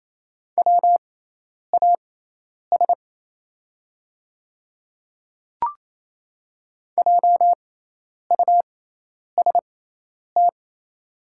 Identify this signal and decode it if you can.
Morse code, heard as a keyed tone, about 28 wpm, spaced as for 10 wpm, 700 Hz. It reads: WAS JUST
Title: 6x Character Spacing - Top 300 Words - 28wpm